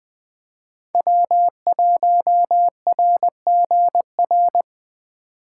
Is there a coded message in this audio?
Yes: Morse code from 0.9 to 4.6 s, heard as a keyed tone, about 20 wpm, 700 Hz. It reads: W1RGR